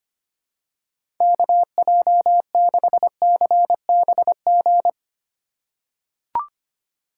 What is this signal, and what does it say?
Morse code, heard as a keyed tone, about 25 wpm, 700 Hz. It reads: KJ6CBG
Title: USA Call Signs - 6 Characters - 2x Character Spacing - 25wpm